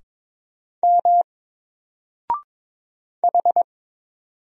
Morse code, heard as a keyed tone, about 22 wpm, 700 Hz. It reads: M H
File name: Single-Letter-Number-22wpm